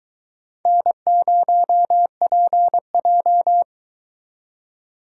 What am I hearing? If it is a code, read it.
Morse code, heard as a keyed tone, about 22 wpm, 700 Hz. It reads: N0PJ